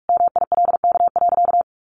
Morse code, heard as a keyed tone, about 44 wpm, 700 Hz. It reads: MILK.
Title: Additional Easy Phrases 45wpm